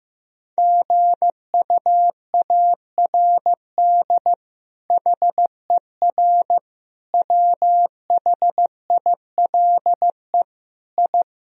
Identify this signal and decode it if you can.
Morse code, heard as a keyed tone, about 15 wpm, 700 Hz. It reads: GUARD HER WHILE I
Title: Sentences from Top 1000 Words - Review Part 4 - 15wpm